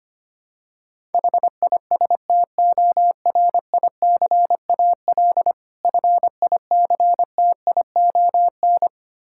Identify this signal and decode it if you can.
Morse code, heard as a keyed tone, about 25 wpm, 700 Hz. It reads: HISTORICAL FICTION
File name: practice / 25wpm / Difficult Words with Hints - 25wpm